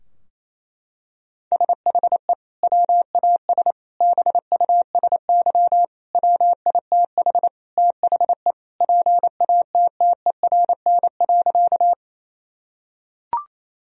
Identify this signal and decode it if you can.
Morse code, heard as a keyed tone, about 28 wpm, 700 Hz. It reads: SHE WAS BUSY WITH THE PATTERN.
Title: Sentences from Top 1000 Words - Review Part 2 - 28wpm